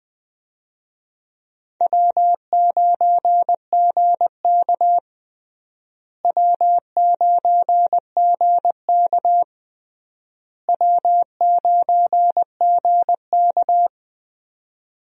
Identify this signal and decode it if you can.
Morse code, heard as a keyed tone, about 21 wpm, 700 Hz. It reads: W9GK W9GK W9GK
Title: USA Call Signs - 4 Characters - Repeated 3x - 20wpm